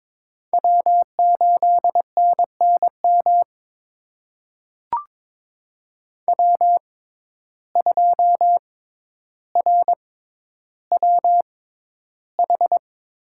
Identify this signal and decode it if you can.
Morse code, heard as a keyed tone, about 22 wpm, spaced as for 11 wpm, 700 Hz. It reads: W8NNM W2RWH
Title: USA Call Signs - 6x Character Spacing - 22wpm